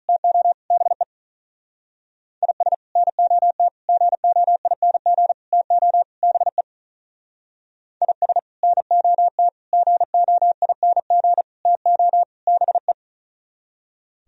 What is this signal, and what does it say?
Morse code, heard as a keyed tone, about 42 wpm, 700 Hz. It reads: TO BE IS NOT GOING TO BE IS NOT GOING TO BE
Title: Speed-Racing - Sets of 5 Words 35wpm